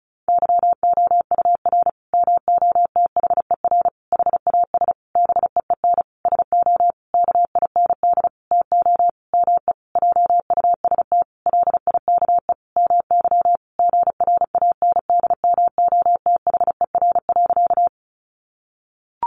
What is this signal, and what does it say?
Morse code, heard as a keyed tone, about 34 wpm, 700 Hz. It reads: YOUR MOTHER HAS BEEN SO KIND TO ME JUST LIKE MY GRANDMOTHER.
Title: Sentences from Top 1200 Words 35wpm